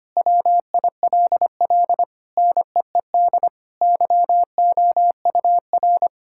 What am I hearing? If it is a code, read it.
Morse code, heard as a keyed tone, about 25 wpm, 700 Hz. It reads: WILL NEED YOUR
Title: Sentences from Top 1600 Words 25wpm